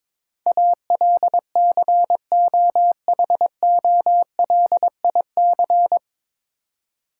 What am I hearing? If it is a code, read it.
Morse code, heard as a keyed tone, about 22 wpm, 700 Hz. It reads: ALCOHOLIC
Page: Morse Code Ninja practice 22wpm